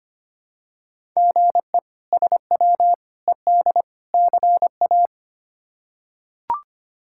Morse code, heard as a keyed tone, about 25 wpm, 700 Hz. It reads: GE SW ED CA
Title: K1USN SST QSO Practice - 25wpm